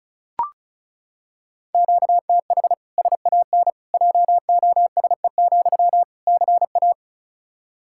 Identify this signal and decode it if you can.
Morse code, heard as a keyed tone, about 36 wpm, 700 Hz. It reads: QTH SAN JOSE, CA
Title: Practice QSO - Exchange 1 - Beginning - 35wpm